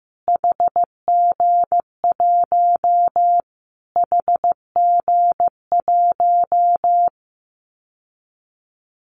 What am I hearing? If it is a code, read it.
Morse code, heard as a keyed tone, about 15 wpm, 700 Hz. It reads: HG1 HG1
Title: Callsign Prefixes for Top 25 DXCC - Repeated 3x - 15wpm